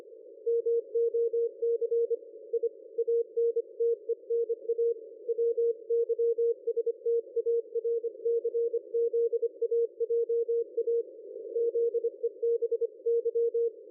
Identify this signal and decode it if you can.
Morse code, heard as a keyed tone, about 25 wpm, 470 Hz. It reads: MOC I ANTENA WYSTARCZAJA ZEBY
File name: dlazaawansowanych / nagraniazpasma / ExpertSDR2 line out 23.01.2020 14.04.01